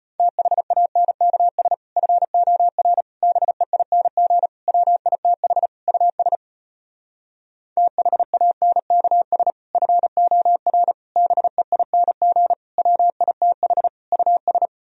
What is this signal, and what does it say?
Morse code, heard as a keyed tone, about 33 wpm, 700 Hz. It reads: THANKS FOR BEING WITH US THANKS FOR BEING WITH US